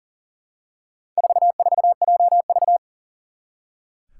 Morse code, heard as a keyed tone, about 40 wpm, 700 Hz. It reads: 44JV
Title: Four Challenging Letter-Number Combinations 40wpm